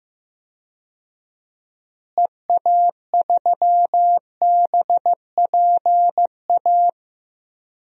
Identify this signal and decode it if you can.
Morse code, heard as a keyed tone, about 15 wpm, 700 Hz. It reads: EA3BPA
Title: DXCC-100 Call Signs - 6 Characters - Repeated 2x - 15wpm